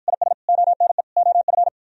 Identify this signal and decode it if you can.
Morse code, heard as a keyed tone, about 53 wpm, 700 Hz. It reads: IS ONE OF